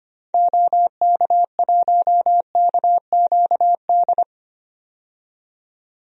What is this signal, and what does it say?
Morse code, heard as a keyed tone, about 25 wpm, 690 Hz. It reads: OK1KQD
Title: DXCC-100 Call Signs - 6 Characters - 25wpm